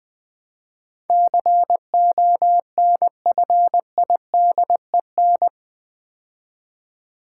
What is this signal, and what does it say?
Morse code, heard as a keyed tone, about 20 wpm, 700 Hz. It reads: CONFIDEN
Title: Build A Word - 4 Letters At A Time - 10-Letter Words - 20wpm